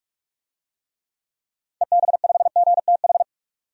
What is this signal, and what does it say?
Morse code, heard as a keyed tone, about 46 wpm, 700 Hz. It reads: EB5GTH
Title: DXCC-100 Call Signs - 6 Characters - 45wpm